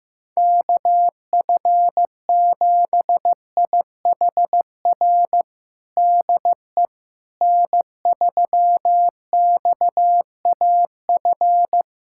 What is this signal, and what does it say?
Morse code, heard as a keyed tone, about 15 wpm, 700 Hz. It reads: KF7IHR DE N3XAF
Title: USA Callsign DE Callsign 15wpm